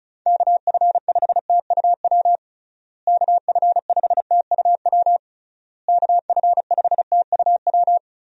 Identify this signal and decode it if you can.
Morse code, heard as a keyed tone, about 34 wpm, 700 Hz. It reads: KF5TUW KF5TUW KF5TUW
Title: USA Call Signs - 6 Characters - Repeated 3x - 35wpm